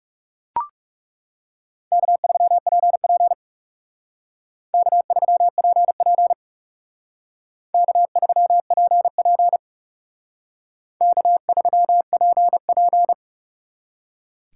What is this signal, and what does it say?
Morse code, heard as a keyed tone, about 38 wpm, 700 Hz. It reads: K3PP K3PP K3PP K3PP